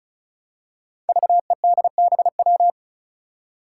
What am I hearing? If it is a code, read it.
Morse code, heard as a keyed tone, about 36 wpm, 700 Hz. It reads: VEDBW
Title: Five Challenging Letter-Number Combinations 35wpm